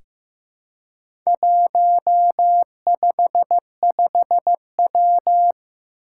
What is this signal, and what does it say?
Morse code, heard as a keyed tone, about 15 wpm, 700 Hz. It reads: 155W
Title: Four Challenging Letter-Number Combinations 15wpm